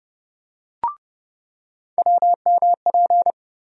Morse code, heard as a keyed tone, about 30 wpm, 700 Hz. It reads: WMP
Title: USA Call Sign Postfixes - 3 Letters 30wpm